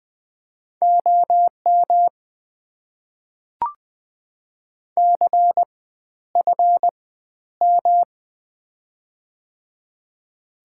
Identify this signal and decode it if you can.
Morse code, heard as a keyed tone, about 21 wpm, spaced as for 11 wpm, 690 Hz. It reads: OM CFM
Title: Top 90 QSO Elements - 4x Character Spacing - 20wpm